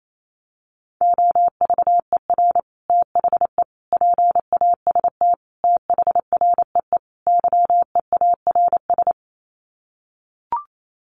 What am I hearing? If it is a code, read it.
Morse code, heard as a keyed tone, about 28 wpm, 700 Hz. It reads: OVER THE PAST THREE YEARS
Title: Sets of 5 Words Encore2 28wpm